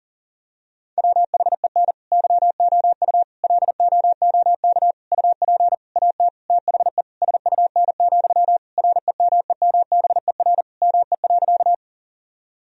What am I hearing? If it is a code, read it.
Morse code, heard as a keyed tone, about 40 wpm, 700 Hz. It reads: WHEN YOU LOOK UP AT THE SUN, REMEMBER ME.
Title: Sentences from Top 1000 Words - Review Part 1 - 40wpm